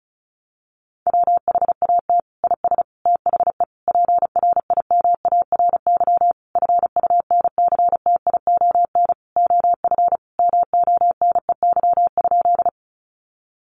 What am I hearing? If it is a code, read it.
Morse code, heard as a keyed tone, about 34 wpm, 700 Hz. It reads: WHAT IS THE PRIMARY FUNCTION OF MONEY?
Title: Sentences from Top 1100 Words 35wpm